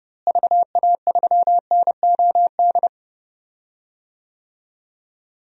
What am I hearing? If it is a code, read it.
Morse code, heard as a keyed tone, about 30 wpm, 700 Hz. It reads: VA3NOD
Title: DXCC-100 Call Signs - 6 Characters - Repeated 2x - 30wpm